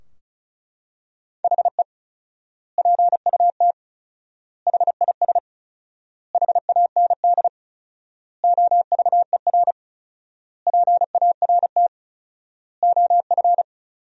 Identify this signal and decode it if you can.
Morse code, heard as a keyed tone, about 36 wpm, 700 Hz. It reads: HE PUT HIS HAND OVER PART OF
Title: 4x Word-Spacing - Sentences from Top 400 Words - 35wpm